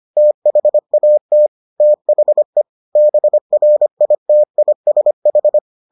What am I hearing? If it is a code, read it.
Morse code, heard as a keyed tone, about 25 wpm, 600 Hz. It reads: THAT THE BRITISH